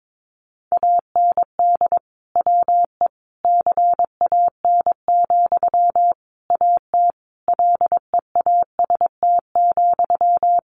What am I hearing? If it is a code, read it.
Morse code, heard as a keyed tone, about 22 wpm, 700 Hz. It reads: AND WE CAN, AT LEAST,
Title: Sentences from Top 900 Words 22wpm